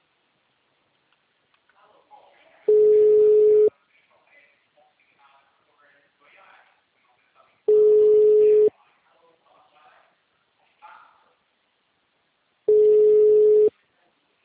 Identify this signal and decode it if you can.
Ringback tone (Europe, 425 Hz), with speech.